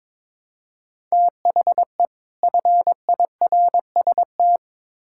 Morse code, heard as a keyed tone, about 22 wpm, 700 Hz. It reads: THE FIRST
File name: Build A Phrase - 2-Word Phrases - 22wpm